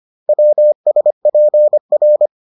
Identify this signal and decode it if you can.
Morse code, heard as a keyed tone, about 25 wpm, 600 Hz. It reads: WSPR